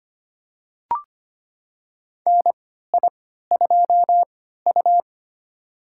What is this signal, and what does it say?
Morse code, heard as a keyed tone, about 25 wpm, spaced as for 14 wpm, 700 Hz. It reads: NI2U
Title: USA Call Signs - 4 Characters - 3x Character Spacing - 25wpm